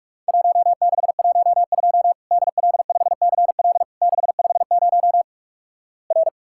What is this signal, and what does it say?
Morse code, heard as a keyed tone, about 44 wpm, 700 Hz. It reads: 1612 DL5CL 650 R
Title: Worked All Europe DX Contest - QTC Exchange - 45wpm